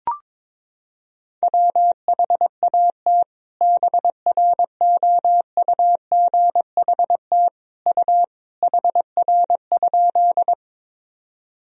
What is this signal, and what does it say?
Morse code, heard as a keyed tone, about 21 wpm, 700 Hz. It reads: WHAT BROUGHT U HR?
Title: Sentences from Top 500 Words and Abbreviations 22wpm